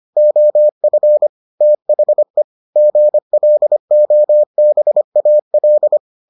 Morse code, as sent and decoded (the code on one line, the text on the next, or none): --- ..-. / - .... / --. .-.. --- -... .- .-..
OF THE GLOBAL